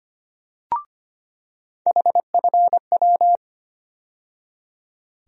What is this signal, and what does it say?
Morse code, heard as a keyed tone, about 25 wpm, 700 Hz. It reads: HFW
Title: USA Call Sign Postfixes - 3 Letters 25wpm